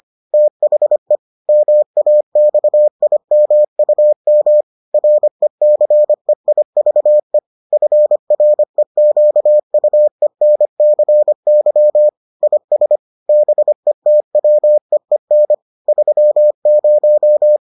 Morse code, heard as a keyed tone, about 25 wpm, 600 Hz. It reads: THE MAXIMUM RECEIVE FREQUENCY IS BETWEEN 30